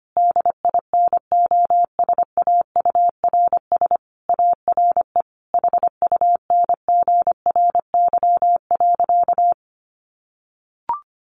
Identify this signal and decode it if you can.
Morse code, heard as a keyed tone, about 25 wpm, 700 Hz. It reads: DINOSAURS ARE HUNGRY.
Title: A Sound Of Thunder 25wpm